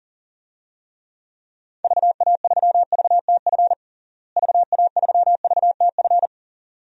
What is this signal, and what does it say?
Morse code, heard as a keyed tone, about 40 wpm, 700 Hz. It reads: VA3VTF VA3VTF
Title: DXCC-100 Call Signs - 6 Characters - Repeated 2x - 40wpm